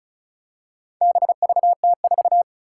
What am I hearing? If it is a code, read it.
Morse code, heard as a keyed tone, about 36 wpm, 700 Hz. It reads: BVT4
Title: Four Challenging Letter-Number Combinations 35wpm